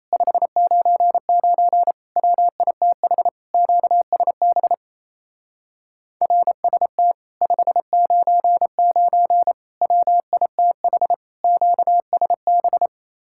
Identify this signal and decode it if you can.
Morse code, heard as a keyed tone, about 33 wpm, 700 Hz. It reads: 599 WITH QSB RST 599 WITH QSB
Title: Speed-Racing Common QSO Phrases - 28wpm